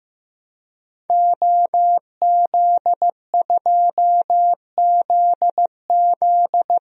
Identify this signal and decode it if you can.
Morse code, heard as a keyed tone, about 15 wpm, 700 Hz. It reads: OZ2ZZ
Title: DXCC-100 Call Signs - 5 Characters - Repeated 2x - 15wpm